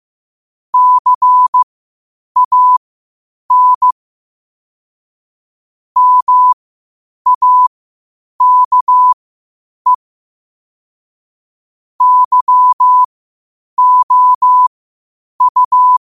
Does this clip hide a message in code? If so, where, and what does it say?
Yes: Morse code from 0.7 to 16.0 s, heard as a keyed tone, about 15 wpm, spaced as for 8 wpm, 1000 Hz. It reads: CAN MAKE YOU